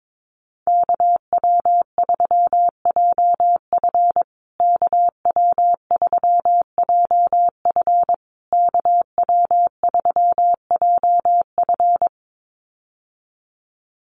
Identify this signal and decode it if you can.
Morse code, heard as a keyed tone, about 22 wpm, 700 Hz. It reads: KW3JF KW3JF KW3JF
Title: USA Call Signs Repeated 3x 22wpm